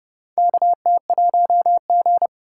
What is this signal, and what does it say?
Morse code, heard as a keyed tone, about 30 wpm, 700 Hz. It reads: KT1G
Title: USA Call Signs - 3x Character Spacing - 30wpm